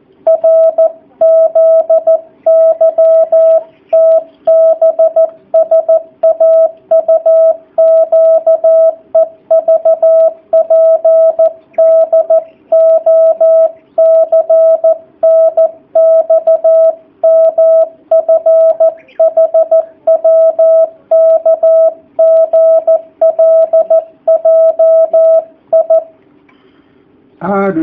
Morse code, heard as a keyed tone, about 14 wpm, 660 Hz. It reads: RZYTBSAUQEVPDOCNXMFHWKGLJI
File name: jh8cbh / 2019cw / sp13- w